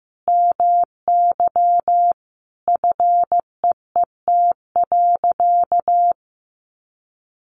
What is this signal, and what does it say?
Morse code, heard as a keyed tone, about 15 wpm, 700 Hz. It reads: MY FEET.
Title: Sentences from Top 1000 Words 15wpm